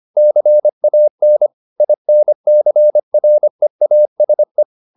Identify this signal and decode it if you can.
Morse code, heard as a keyed tone, about 25 wpm, 600 Hz. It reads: CAN INCREASE
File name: the old man and the sea part 1 25wpm